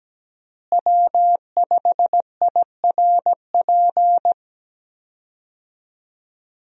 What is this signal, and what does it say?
Morse code, heard as a keyed tone, about 17 wpm, 700 Hz. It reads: W5IRP
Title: USA Call Signs - 5 Characters - Repeated 3x - 17wpm